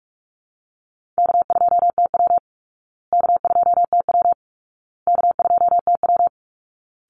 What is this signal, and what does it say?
Morse code, heard as a keyed tone, about 44 wpm, 700 Hz. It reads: K2TW K2TW K2TW
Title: Easy Call Signs Repeated 3x - 45wpm